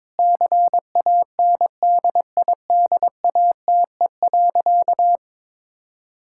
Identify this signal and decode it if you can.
Morse code, heard as a keyed tone, about 22 wpm, 700 Hz. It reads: CANDIDATE.
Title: Sentences from Top 1600 Words 22wpm